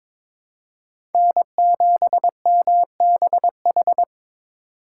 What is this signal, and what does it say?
Morse code, heard as a keyed tone, about 22 wpm, 700 Hz. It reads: N7MBH
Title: USA Call Signs - 5 Characters - 4x Character Spacing - 22wpm